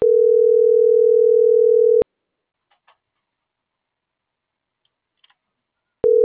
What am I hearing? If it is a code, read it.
Ringback tone (North America), 440 + 480 Hz.